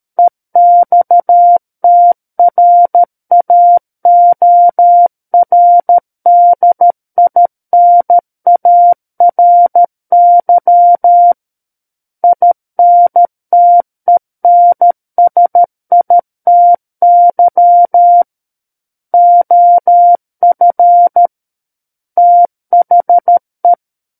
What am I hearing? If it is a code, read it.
Morse code, heard as a keyed tone, about 13 wpm, 700 Hz. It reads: EXTRAORDINARY INTENSITY OF THE
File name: Book One / 04-Chapter 4 - 13 WPM